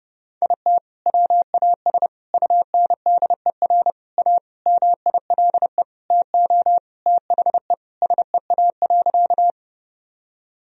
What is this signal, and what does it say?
Morse code, heard as a keyed tone, about 30 wpm, 700 Hz. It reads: IT WAS UNDER A MILE TO THE SEA.
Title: Sentences from Top 1000 Words - Review Part 1 - 30wpm